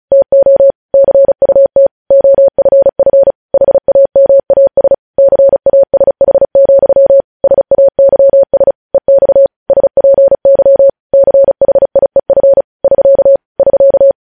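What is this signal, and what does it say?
Morse code, heard as a keyed tone, about 36 wpm, 550 Hz. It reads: TO CUT OFF HAMAS CASH, SAYS EX SPY CHIEF <SK> <SK>
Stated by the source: Morse code news headlines practice at 35.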